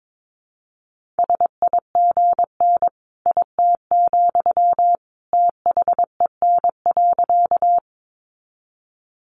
Morse code, heard as a keyed tone, about 22 wpm, 700 Hz. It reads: SIGN IT, THEN.
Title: Sentences from Top 1000 Words 22wpm